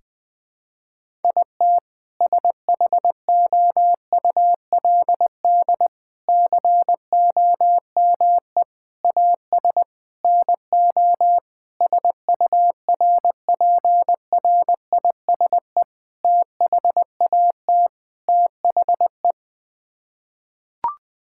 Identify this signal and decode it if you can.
Morse code, heard as a keyed tone, about 20 wpm, 700 Hz. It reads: IT SHOULD COME AS NO SURPRISE THAT THE
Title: Sets of 8 Words - 20wpm